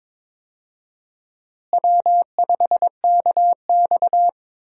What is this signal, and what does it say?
Morse code, heard as a keyed tone, about 22 wpm, 700 Hz. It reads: W5KX
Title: USA Call Signs - 4 Characters - Repeated 3x - 22wpm